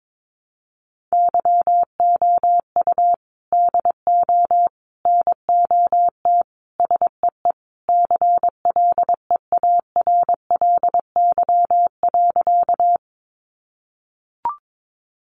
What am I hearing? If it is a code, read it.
Morse code, heard as a keyed tone, about 22 wpm, 700 Hz. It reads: YOU DO NOT SEE CLEARLY.
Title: Sentences from Top 1300 Words 22wpm